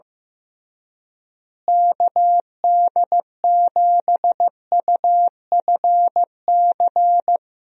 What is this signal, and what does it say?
Morse code, heard as a keyed tone, about 15 wpm, 700 Hz. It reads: KD7UFC
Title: USA Call Signs - 6 Characters - Repeated 3x - 15wpm